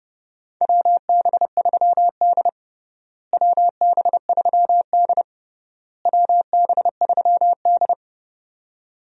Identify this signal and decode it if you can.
Morse code, heard as a keyed tone, about 30 wpm, 700 Hz. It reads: WB3D WB3D WB3D
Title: USA Call Signs - 4 Characters - Repeated 3x - 30wpm